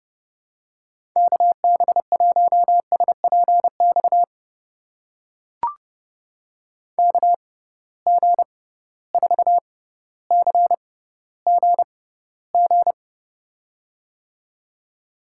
Morse code, heard as a keyed tone, about 30 wpm, spaced as for 15 wpm, 700 Hz. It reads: KB1SPX KG4CGG
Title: USA Call Signs - 6 Characters - 6x Character Spacing - 30wpm